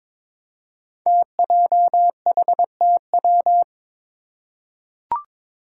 Morse code, heard as a keyed tone, about 23 wpm, 700 Hz. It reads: TJHTW
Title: Five Challenging Letter-Number Combinations 22wpm